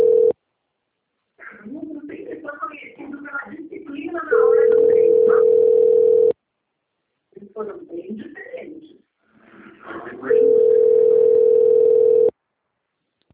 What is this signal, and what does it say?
Ringback tone (North America), 440 + 480 Hz, with speech.
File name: rg-1600-unknown- - - .30213